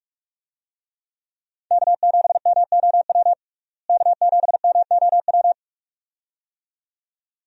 Morse code, heard as a keyed tone, about 41 wpm, 700 Hz. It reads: K7MOW K7MOW
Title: USA Call Signs - 5 Characters - Repeated 2x - 45wpm